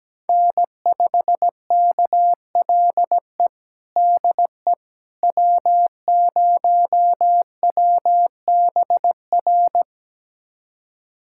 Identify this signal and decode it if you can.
Morse code, heard as a keyed tone, about 17 wpm, 700 Hz. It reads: N5KLE DE W0WBR
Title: USA Callsign DE Callsign - 5 Characters - 17wpm